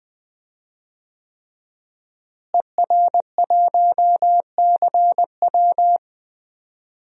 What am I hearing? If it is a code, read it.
Morse code, heard as a keyed tone, about 20 wpm, 700 Hz. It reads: ER1CW
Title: DXCC-100 Call Signs - 5 Characters - Repeated 3x - 20wpm